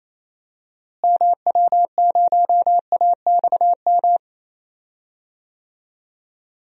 Morse code, heard as a keyed tone, about 28 wpm, 700 Hz. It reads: MW0AXM
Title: DXCC-100 Call Signs - 6 Characters - Repeated 2x - 28wpm